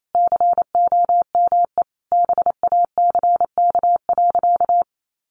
Morse code, heard as a keyed tone, about 28 wpm, 700 Hz. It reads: COME BACK.
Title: Sentences from Top 1500 Words 28wpm